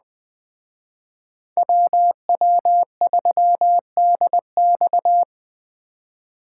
Morse code, heard as a keyed tone, about 20 wpm, 700 Hz. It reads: WW3DX
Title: Speed Racing - USA 2x2 Call Signs 17wpm